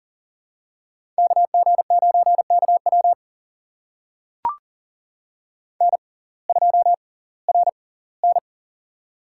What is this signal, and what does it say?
Morse code, heard as a keyed tone, about 40 wpm, spaced as for 22 wpm, 700 Hz. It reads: KG9KW N2RN
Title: USA Call Signs - 6x Character Spacing - 40wpm